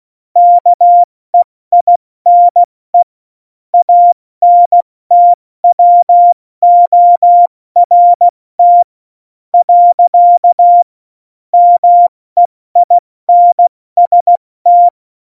Morse code, heard as a keyed tone, about 16 wpm, 700 Hz. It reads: KEINE ANTWORT . MEINST
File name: Morsekurs / Witze / Witz 14. August 2018, 70 F80